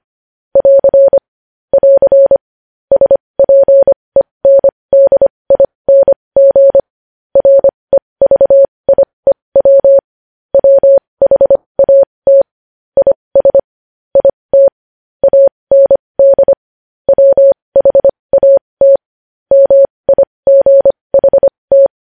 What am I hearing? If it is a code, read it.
Morse code, heard as a keyed tone, about 25 wpm, spaced as for 20 wpm, 550 Hz. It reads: <AR> <AR> SPENDING REVIEW WHAT IS IT AND WHAT MIGHT
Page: news-headlines-in-morse-code-at-20-wpm